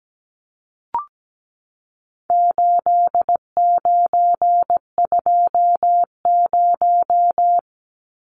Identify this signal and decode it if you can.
Morse code, heard as a keyed tone, about 17 wpm, 700 Hz. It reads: 8920
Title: Numbers - 4-Digit 17wpm